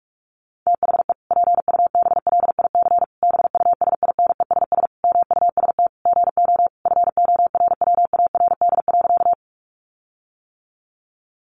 Morse code, heard as a keyed tone, about 44 wpm, 700 Hz. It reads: THE PUBLIC BUSINESS MUST GO FORWARD.
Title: Sentences from Top 1300 Words 45wpm